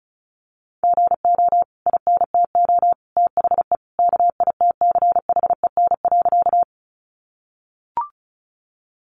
Morse code, heard as a keyed tone, about 34 wpm, 700 Hz. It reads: GO INTO THE KITCHEN.
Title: Sentences from Top 1400 Words 35wpm